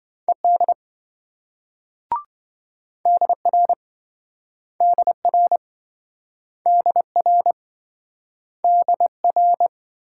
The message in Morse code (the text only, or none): ED DR DR DR DR